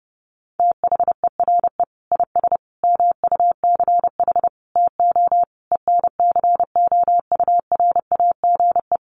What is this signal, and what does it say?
Morse code, heard as a keyed tone, about 30 wpm, 700 Hz. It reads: THERE IS MUCH TO ENCOURAGE
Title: Sentences from Top 1400 Words 30wpm